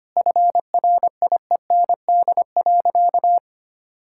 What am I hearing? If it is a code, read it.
Morse code, heard as a keyed tone, about 25 wpm, 700 Hz. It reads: FRIEND.
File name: practice / 25wpm / Sentences with 2-Word Phrases - 25wpm